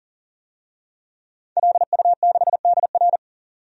Morse code, heard as a keyed tone, about 40 wpm, 700 Hz. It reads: LU6DR
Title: DXCC-100 Call Signs - 5 Characters - 40wpm